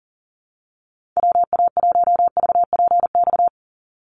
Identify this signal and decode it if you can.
Morse code, heard as a keyed tone, about 40 wpm, 700 Hz. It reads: WA1VPX
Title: USA Call Signs 40wpm